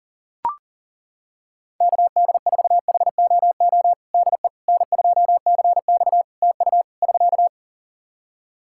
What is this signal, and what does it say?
Morse code, heard as a keyed tone, about 37 wpm, 700 Hz. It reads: KD4HOO DE N2CX TU <SK>
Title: Practice QSO - Exchange 4 - Ending - 40wpm